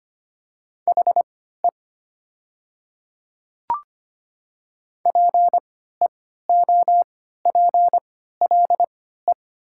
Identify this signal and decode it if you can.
Morse code, heard as a keyed tone, about 25 wpm, spaced as for 14 wpm, 700 Hz. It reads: HE PEOPLE
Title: 3x Character Spacing - Top 100 Words - 25wpm